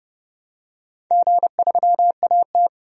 Morse code, heard as a keyed tone, about 29 wpm, 700 Hz. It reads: G3AT
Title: DXCC-100 Call Signs - 4 Characters - 30wpm